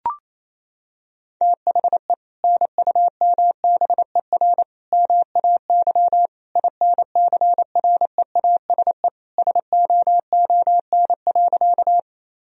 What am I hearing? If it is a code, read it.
Morse code, heard as a keyed tone, about 27 wpm, 700 Hz. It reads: THE NUMBER MAY INCREASE SOON.
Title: Sentences from Top 1000 Words - Review Part 4 - 28wpm